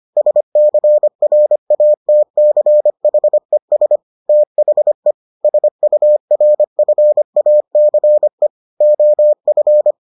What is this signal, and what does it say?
Morse code, heard as a keyed tone, about 25 wpm, 600 Hz. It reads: SCRATCHES THE SURFACE OF